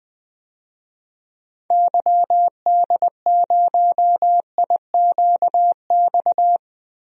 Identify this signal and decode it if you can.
Morse code, heard as a keyed tone, about 20 wpm, 700 Hz. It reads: YD0IQX
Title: DXCC-100 Call Signs - 6 Characters - 20wpm